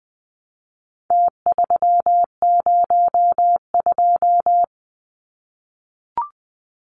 Morse code, heard as a keyed tone, about 20 wpm, 700 Hz. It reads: T302
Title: Numbers - Contest Style - 4-Digit 20wpm